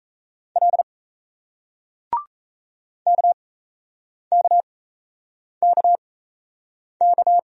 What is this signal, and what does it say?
Morse code, heard as a keyed tone, about 37 wpm, spaced as for 8 wpm, 700 Hz. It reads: L KKKK